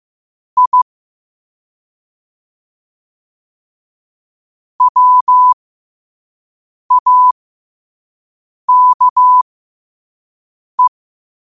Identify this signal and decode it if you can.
Morse code, heard as a keyed tone, about 15 wpm, spaced as for 5 wpm, 1000 Hz. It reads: I WAKE
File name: qotdfiles / 5wpm